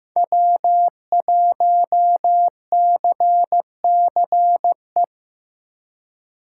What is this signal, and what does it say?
Morse code, heard as a keyed tone, about 15 wpm, 700 Hz. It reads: W1CCE
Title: K1USN SST QSO Practice - 15wpm